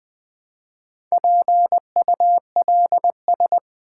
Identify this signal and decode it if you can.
Morse code, heard as a keyed tone, about 20 wpm, 700 Hz. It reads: PULS
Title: Build A Word - 1 Letter At A Time - 3 to 5-Letter Words - 20wpm